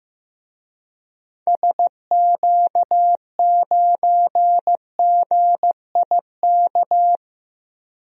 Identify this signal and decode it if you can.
Morse code, heard as a keyed tone, about 15 wpm, 700 Hz. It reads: SQ9GIK
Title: DXCC-100 Call Signs - 6 Characters - Repeated 3x - 15wpm